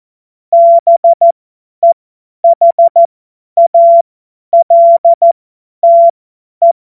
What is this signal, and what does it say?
Morse code, heard as a keyed tone, about 14 wpm, spaced as for 10 wpm, 680 Hz. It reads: BEHALTE
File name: Morsekurs / Witze / Witz 21.November 2023 50 F70